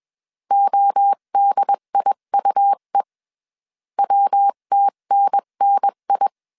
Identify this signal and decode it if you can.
Morse code, heard as a keyed tone, about 21 wpm, 800 Hz. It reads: ODIUE WTNNI
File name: 20wpm